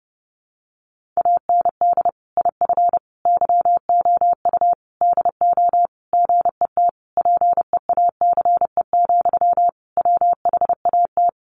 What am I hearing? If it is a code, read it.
Morse code, heard as a keyed tone, about 30 wpm, 700 Hz. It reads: AND IF YOU DO GET PEACE, WHAT
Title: Sentences from Top 1500 Words 30wpm